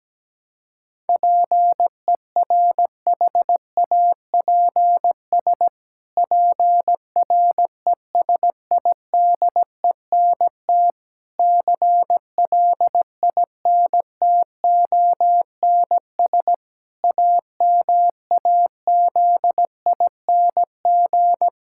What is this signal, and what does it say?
Morse code, heard as a keyed tone, about 17 wpm, 700 Hz. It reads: PERHAPS PRESIDENT CLINTONS AMAZING
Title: Pangram Challenge v2 - 17wpm